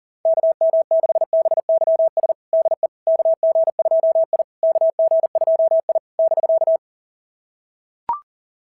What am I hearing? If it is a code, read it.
Morse code, heard as a keyed tone, about 40 wpm, 660 Hz. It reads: KM6BYS DE KG2I KG2I <BK>
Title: SOTA QSO Practice 40wpm